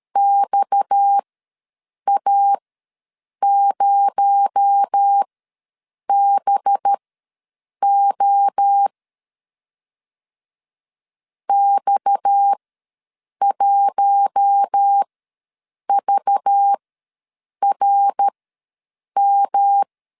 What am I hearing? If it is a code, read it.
Morse code, heard as a keyed tone, about 13 wpm, spaced as for 7 wpm, 800 Hz. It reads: XA0BO X1VRM
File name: LEZIONI / 7 GF9Z / 7 G ws 12wpm 3140